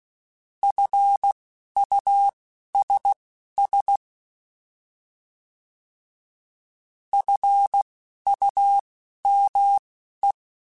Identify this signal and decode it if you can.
Morse code, heard as a keyed tone, about 16 wpm, spaced as for 11 wpm, 800 Hz. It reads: FUSS FUME
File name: Alphabetical / Alphabetical mp3s / Fs @10